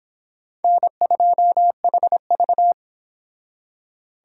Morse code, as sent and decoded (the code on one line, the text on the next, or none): -. ..--- .... ...-
N2HV